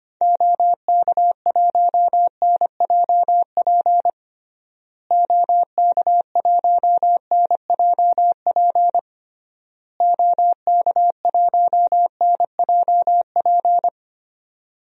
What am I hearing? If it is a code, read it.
Morse code, heard as a keyed tone, about 25 wpm, 700 Hz. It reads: OK1NJP OK1NJP OK1NJP
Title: DXCC-100 Call Signs - 6 Characters - Repeated 3x - 25wpm